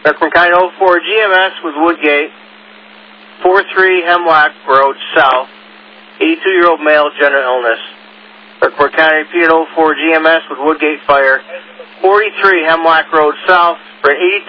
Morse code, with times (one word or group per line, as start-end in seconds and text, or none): none